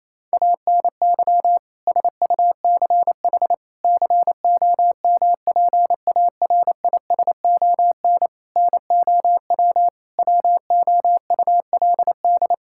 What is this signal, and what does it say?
Morse code, heard as a keyed tone, about 28 wpm, 700 Hz. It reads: ANY SUCH COMPARISON NOW WOULD
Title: Sentences from Top 1600 Words 28wpm